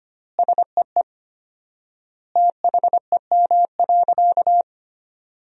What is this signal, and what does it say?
Morse code, heard as a keyed tone, about 25 wpm, 700 Hz. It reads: SEE THEM.
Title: 4x Word-Spacing - Sentences from Top 300 Words - 25wpm